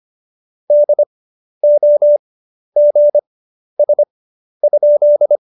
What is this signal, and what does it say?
Morse code, heard as a keyed tone, about 25 wpm, spaced as for 11 wpm, 590 Hz. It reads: DOGS?
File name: Jokes-25-char-10-eff